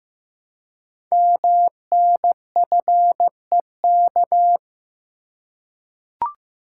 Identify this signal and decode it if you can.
Morse code, heard as a keyed tone, about 15 wpm, 700 Hz. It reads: MNFEK
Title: Code Groups - 5 Letters - 15wpm